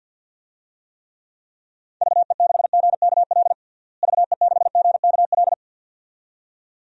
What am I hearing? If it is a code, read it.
Morse code, heard as a keyed tone, about 52 wpm, 700 Hz. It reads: VE6GKL VE6GKL
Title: DXCC-100 Call Signs - 6 Characters - Repeated 2x - 50wpm